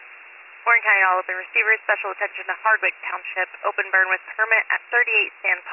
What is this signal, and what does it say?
Emergency Alert System (SAME) tones, heard as harsh buzzing bursts, with speech. It is heard from 0.6 to 4.7 s.